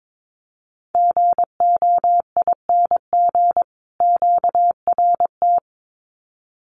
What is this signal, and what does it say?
Morse code, heard as a keyed tone, about 22 wpm, 700 Hz. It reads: GOING QRT
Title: Common QSO Phrases 22wpm